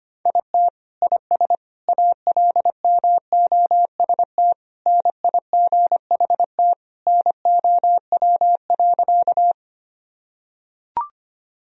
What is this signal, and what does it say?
Morse code, heard as a keyed tone, about 25 wpm, 700 Hz. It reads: IT IS ALMOST NIGHT NOW.
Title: Sentences from Top 1000 Words - Review Part 4 - 25wpm